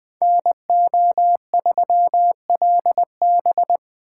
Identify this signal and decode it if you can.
Morse code, heard as a keyed tone, about 24 wpm, 700 Hz. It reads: NO3LB
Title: Speed Racing - USA Call Signs 20wpm